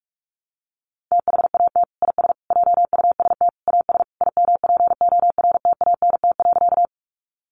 Morse code, heard as a keyed tone, about 44 wpm, 700 Hz. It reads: THAT IS JUST AS IMPORTANT.
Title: Sentences from Top 600 Words 45wpm